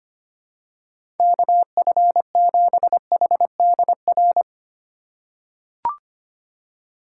Morse code, heard as a keyed tone, about 25 wpm, 700 Hz. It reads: KF7HDR
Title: USA Call Signs - 2x Character Spacing - 25wpm